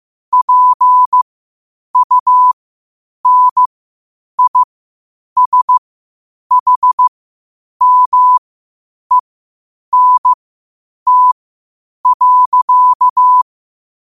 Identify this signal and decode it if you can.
Morse code, heard as a keyed tone, about 15 wpm, spaced as for 8 wpm, 1000 Hz. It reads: PUNISHMENT.
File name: qotdfiles / 8wpm